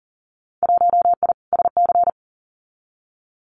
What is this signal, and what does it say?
Morse code, heard as a keyed tone, about 40 wpm, 700 Hz. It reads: 1I SC
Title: Winter Field Day Exchanges Encore 40wpm